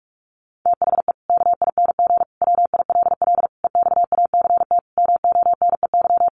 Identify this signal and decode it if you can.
Morse code, heard as a keyed tone, about 44 wpm, 700 Hz. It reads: THE KING WILL EXACT MONEY
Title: Sentences from Top 1300 Words 45wpm